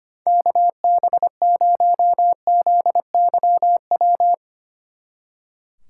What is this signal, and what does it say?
Morse code, heard as a keyed tone, about 31 wpm, 700 Hz. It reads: KB0ZYW
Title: Speed Racing - USA 2x3 Call Signs 25wpm